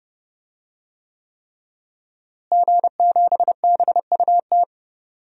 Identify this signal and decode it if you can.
Morse code, heard as a keyed tone, about 32 wpm, 700 Hz. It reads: G7BUT
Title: DXCC-100 Call Signs - 5 Characters - Repeated 3x - 30wpm